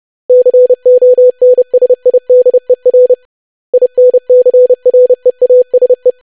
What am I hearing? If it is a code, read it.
Morse code, heard as a keyed tone, about 30 wpm, 500 Hz. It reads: CONSIDER INCREASE
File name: lesson / web / basic500-25wpm